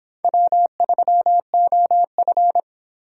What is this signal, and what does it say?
Morse code, heard as a keyed tone, about 31 wpm, 700 Hz. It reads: W3OF